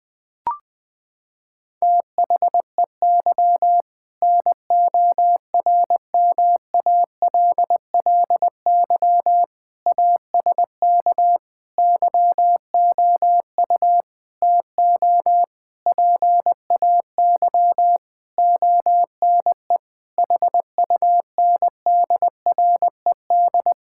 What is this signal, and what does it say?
Morse code, heard as a keyed tone, about 21 wpm, 700 Hz. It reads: THEY NORMALLY ASK YOU TO PAY ONE HUNDRED
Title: Sentences from Top 1000 Words - Review Part 6 - 20wpm